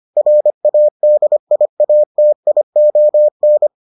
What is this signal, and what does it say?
Morse code, heard as a keyed tone, about 25 wpm, 600 Hz. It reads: RADIATION